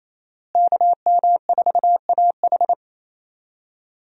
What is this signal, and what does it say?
Morse code, heard as a keyed tone, about 28 wpm, 700 Hz. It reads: KM4AH
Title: USA Call Signs - 5 Characters - Repeated 2x - 28wpm